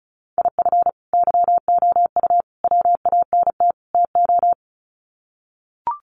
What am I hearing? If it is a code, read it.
Morse code, heard as a keyed tone, about 34 wpm, 700 Hz. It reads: IF YOU WANT TO
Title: Sets of 4 Words 35wpm